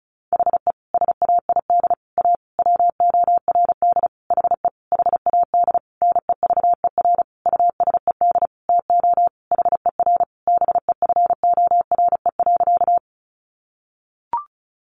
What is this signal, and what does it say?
Morse code, heard as a keyed tone, about 34 wpm, 700 Hz. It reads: HE SAID A WORD HE HAD NEVER USED TO HER BEFORE.
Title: Sentences from Top 600 Words 35wpm